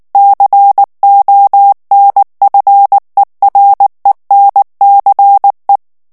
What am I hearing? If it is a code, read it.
Morse code, heard as a keyed tone, about 19 wpm, 800 Hz. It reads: CONFERENCE
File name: ANGLAIS texte clair 3 vitesse 18 mots minute